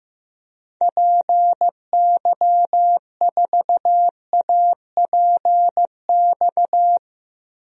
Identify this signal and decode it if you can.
Morse code, heard as a keyed tone, about 15 wpm, 700 Hz. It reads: PY4APX
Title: DXCC-100 Call Signs - 6 Characters - Repeated 3x - 15wpm